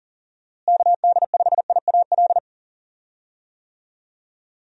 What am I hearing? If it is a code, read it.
Morse code, heard as a keyed tone, about 39 wpm, 700 Hz. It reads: KD5IAL
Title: USA Call Signs - 6 Characters - Repeated 2x - 40wpm